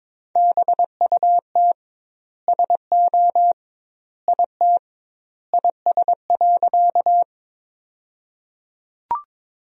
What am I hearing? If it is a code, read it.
Morse code, heard as a keyed tone, about 21 wpm, 700 Hz. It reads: BUT SO IT IS.
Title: 2x Word-Spacing - Sentences from Top 100 Words - 22wpm